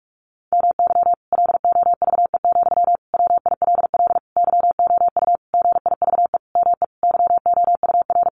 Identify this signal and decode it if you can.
Morse code, heard as a keyed tone, about 44 wpm, 700 Hz. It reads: MY LOVE, WILL YOU GIVE ME YOUR
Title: Sentences from Top 1200 Words 45wpm